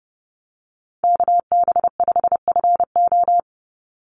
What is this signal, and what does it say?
Morse code, heard as a keyed tone, about 30 wpm, 700 Hz. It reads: KB5FO
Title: USA Call Signs 30wpm